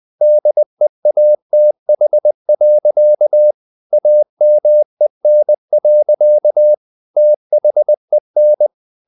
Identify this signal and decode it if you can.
Morse code, heard as a keyed tone, about 20 wpm, 600 Hz. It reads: DEATH. AMEN. THEN
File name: the old man and the sea / the old man and the sea part 1 20wpm